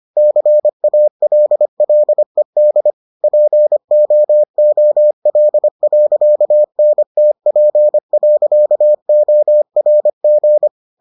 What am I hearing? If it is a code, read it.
Morse code, heard as a keyed tone, about 25 wpm, 600 Hz. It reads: CALLED POOL.NTP.ORG